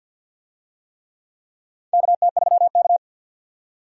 Morse code, heard as a keyed tone, about 46 wpm, 700 Hz. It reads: KT3K
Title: USA Call Signs - 4 Characters - 50wpm